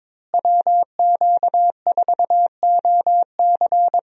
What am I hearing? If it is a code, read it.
Morse code, heard as a keyed tone, about 22 wpm, 700 Hz. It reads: WQ4OC
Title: USA Call Signs - 5 Characters - Repeated 3x - 22wpm